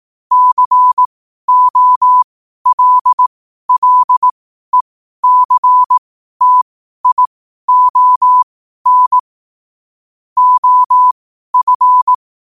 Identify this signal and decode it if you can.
Morse code, heard as a keyed tone, about 18 wpm, spaced as for 12 wpm, 1000 Hz. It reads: COLLECTION OF
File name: qotdfiles / 12wpm